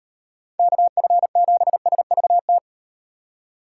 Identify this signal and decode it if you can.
Morse code, heard as a keyed tone, about 36 wpm, 700 Hz. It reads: KF7SVT